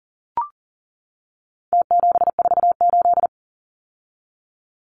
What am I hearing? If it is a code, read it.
Morse code, heard as a keyed tone, about 40 wpm, 700 Hz. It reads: T748
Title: Numbers - Contest Style - 4-Digit 40wpm